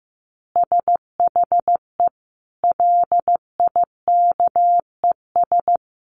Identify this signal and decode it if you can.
Morse code, heard as a keyed tone, about 15 wpm, 700 Hz. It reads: SHE LIKES
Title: Easy Phrases 15wpm